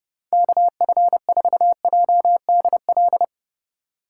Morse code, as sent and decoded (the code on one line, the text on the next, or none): -.- ..-. ....- .--- -.. .-..
KF4JDL